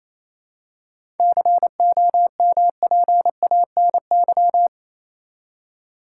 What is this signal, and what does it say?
Morse code, heard as a keyed tone, about 28 wpm, 700 Hz. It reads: COMPANY
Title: Seven Letter Words - 28wpm